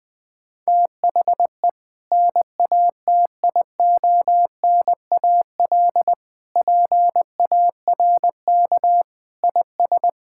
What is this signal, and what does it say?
Morse code, heard as a keyed tone, about 20 wpm, 700 Hz. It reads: THE NATIONAL PARK IS
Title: Sentences with 2-Word Phrases - Spoken Beginning - 20wpm